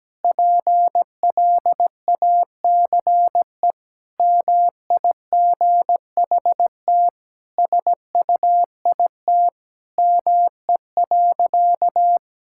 Morse code, heard as a keyed tone, about 17 wpm, 700 Hz. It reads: PLACE MIGHT SUIT ME.
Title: Sentences from Top 1600 Words 17wpm